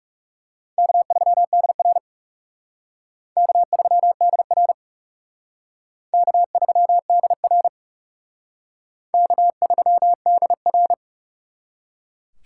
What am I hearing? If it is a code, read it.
Morse code, heard as a keyed tone, about 36 wpm, 700 Hz. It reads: K3DR K3DR K3DR K3DR
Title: Speed Racing - USA 1x2 Call Signs 30wpm